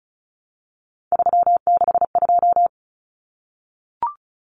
Morse code, heard as a keyed tone, about 34 wpm, 700 Hz. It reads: 362
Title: Common Cut Numbers - 3-Digit 35wpm